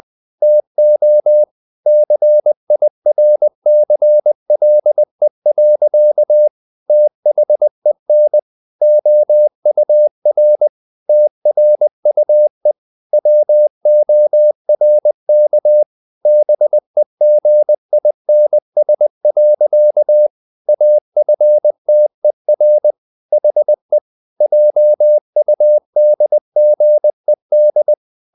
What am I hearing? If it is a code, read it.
Morse code, heard as a keyed tone, about 20 wpm, 600 Hz. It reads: TO CIRCLE. THEN OUR TRUE WORK BEGINS. AFTER HE JUDGED
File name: the old man and the sea part 2 20wpm